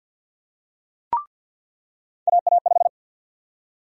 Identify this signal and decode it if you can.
Morse code, heard as a keyed tone, about 48 wpm, 700 Hz. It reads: AA5
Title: USA Call Sign Prefixes - 50wpm